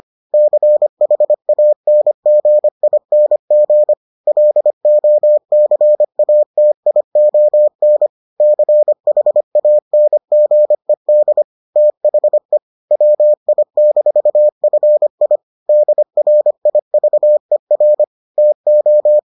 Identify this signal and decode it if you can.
Morse code, heard as a keyed tone, about 25 wpm, 600 Hz. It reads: CHANGING LOCATION CHANGED THE WI-FI DRIVER TO